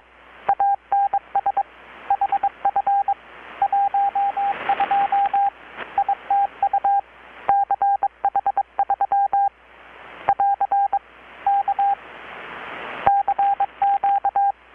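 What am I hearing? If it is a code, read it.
Morse code, heard as a keyed tone, about 22 wpm, 800 Hz, with radio noise. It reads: ANS HF 12 ITU CH3 <AR> K CQ
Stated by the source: Morse from KPH's Night of Nights XX maritime radiotelegraphy event, received in Chiapas, Mexico about 3,800 km away on a Kenwood R-600 with a grounded 10 m long-wire antenna.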